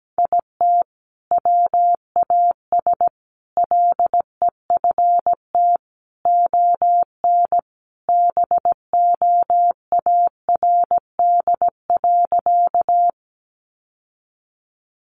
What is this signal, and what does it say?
Morse code, heard as a keyed tone, about 17 wpm, 700 Hz. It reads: IT WAS LEFT ON BOARD.
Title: Sentences from Top 700 Words 17wpm